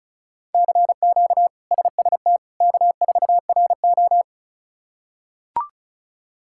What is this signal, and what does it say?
Morse code, heard as a keyed tone, about 36 wpm, 700 Hz. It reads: CQ SST K4RO
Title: K1USN SST QSO Practice - 35wpm